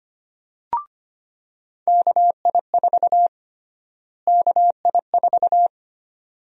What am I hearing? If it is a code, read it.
Morse code, heard as a keyed tone, about 25 wpm, 700 Hz. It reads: KI4 KI4
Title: USA Call Sign Prefixes - Repeated 2x - 25wpm